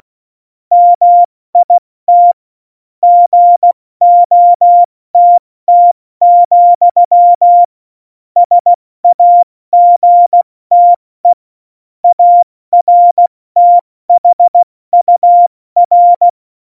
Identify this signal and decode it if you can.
Morse code, heard as a keyed tone, about 16 wpm, 700 Hz. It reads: MIT GOTT, SAGTE ARTHUR
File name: Morsekurs / Witze / Witz 26. Januar 2021, 70 F80